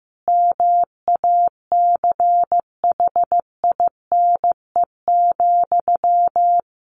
Morse code, heard as a keyed tone, about 15 wpm, 700 Hz. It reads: MACHINE,
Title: A Sound Of Thunder 15wpm